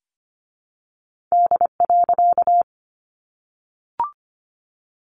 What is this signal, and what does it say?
Morse code, heard as a keyed tone, about 25 wpm, 700 Hz. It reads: D.
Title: A Sound Of Thunder 25wpm